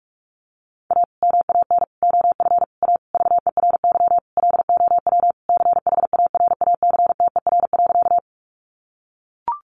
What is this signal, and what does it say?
Morse code, heard as a keyed tone, about 44 wpm, 700 Hz. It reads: A MAN OF A VERY LOW CHARACTER.
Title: Sentences from Top 1200 Words 45wpm